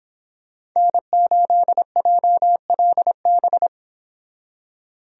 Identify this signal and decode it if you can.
Morse code, heard as a keyed tone, about 26 wpm, 700 Hz. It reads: N8JLB
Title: Speed Racing - USA 1x3 Call Signs 22wpm